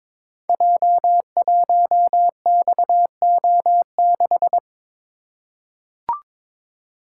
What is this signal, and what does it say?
Morse code, heard as a keyed tone, about 22 wpm, 700 Hz. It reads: J1XO6
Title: Code Groups - 5 Letter:Numbers - 22wpm